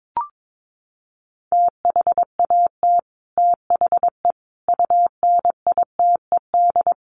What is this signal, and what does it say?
Morse code, heard as a keyed tone, about 22 wpm, 700 Hz. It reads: THAT THE UNITED
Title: Sets of 5 Words Encore 22wpm